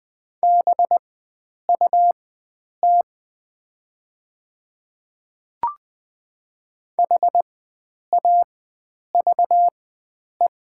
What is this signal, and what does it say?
Morse code, heard as a keyed tone, about 20 wpm, spaced as for 9 wpm, 690 Hz. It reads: BUT HAVE
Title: 4x Character Spacing - Top 100 Words - 20wpm